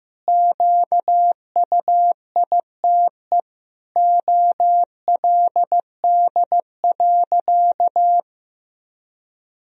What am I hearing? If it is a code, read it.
Morse code, heard as a keyed tone, about 15 wpm, 700 Hz. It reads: QUITE OLD.
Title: Sentences from Top 1000 Words - Review Part 3 - 15wpm